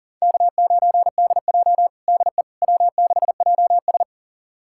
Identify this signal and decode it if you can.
Morse code, heard as a keyed tone, about 40 wpm, 700 Hz. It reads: K9DJ DE W6JS
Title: USA Callsign DE Callsign - 4 Characters - 40wpm